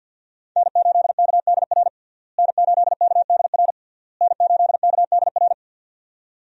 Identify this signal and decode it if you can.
Morse code, heard as a keyed tone, about 49 wpm, 700 Hz. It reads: N8KDR N8KDR N8KDR
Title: USA Call Signs - 5 Characters - Repeated 3x - 50wpm